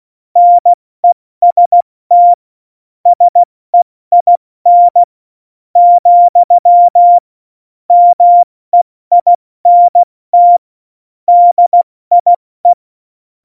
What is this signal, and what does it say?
Morse code, heard as a keyed tone, about 16 wpm, spaced as for 13 wpm, 700 Hz. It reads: NEST SEIN, MEINT DIE
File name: Morsekurs / Witze / MP3, Witz 08. Mai 2018, 70 F80